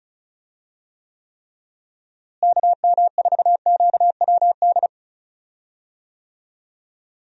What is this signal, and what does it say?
Morse code, heard as a keyed tone, about 34 wpm, 700 Hz. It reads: KM4QWD
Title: USA Call Signs - 6 Characters - Repeated 3x - 35wpm